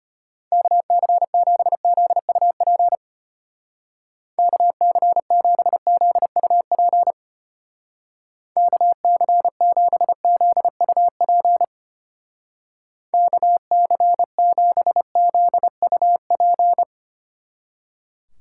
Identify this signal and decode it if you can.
Morse code, heard as a keyed tone, about 31 wpm, 700 Hz. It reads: KC7ZUP KC7ZUP KC7ZUP KC7ZUP
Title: Speed Racing - USA 2x3 Call Signs 25wpm